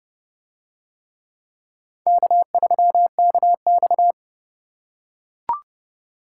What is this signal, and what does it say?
Morse code, heard as a keyed tone, about 28 wpm, 700 Hz. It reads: K3KX